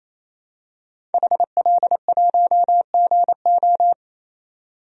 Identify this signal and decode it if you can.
Morse code, heard as a keyed tone, about 27 wpm, 700 Hz. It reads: HL1GO
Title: DXCC-100 Call Signs - 5 Characters - 28wpm